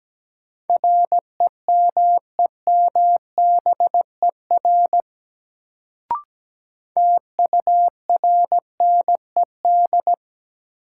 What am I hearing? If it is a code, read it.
Morse code, heard as a keyed tone, about 17 wpm, 700 Hz. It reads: REMEMBER TURNED
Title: Sets of 4 Words - New Word Drill - 17wpm